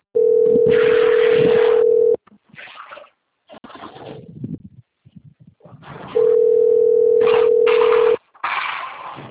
Ringback tone (North America), 440 + 480 Hz.
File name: out- - - - - .44243